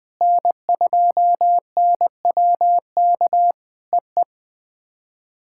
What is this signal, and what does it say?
Morse code, heard as a keyed tone, about 20 wpm, 700 Hz. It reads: N2NWK EE
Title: POTA QSO Practice - 20wpm